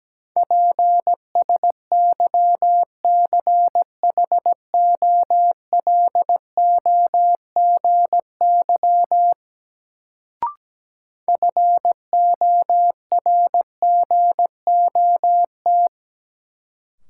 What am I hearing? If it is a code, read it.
Morse code, heard as a keyed tone, about 17 wpm, 700 Hz. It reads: PSYCHOLOGY FORGOT
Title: Sentences from Top 1400 Words - New Word Drill - 17wpm